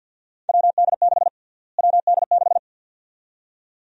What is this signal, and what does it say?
Morse code, heard as a keyed tone, about 49 wpm, 700 Hz. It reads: WD6 WD6
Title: USA Call Sign Prefixes - Repeated 2x - 50wpm